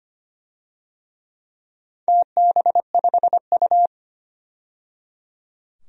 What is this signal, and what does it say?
Morse code, heard as a keyed tone, about 25 wpm, 700 Hz. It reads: TB5U
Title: DXCC-100 Call Signs - 4 Characters - 25wpm